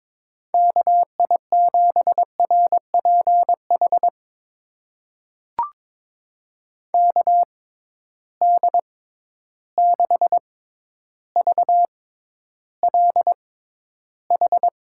Morse code, heard as a keyed tone, about 22 wpm, spaced as for 11 wpm, 700 Hz. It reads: KI7RPH KD6VLH
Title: USA Call Signs - 6 Characters - 6x Character Spacing - 22wpm